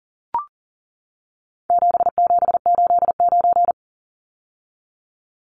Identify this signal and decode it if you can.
Morse code, heard as a keyed tone, about 38 wpm, 700 Hz. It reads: 7789
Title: Numbers - 4-Digit 40wpm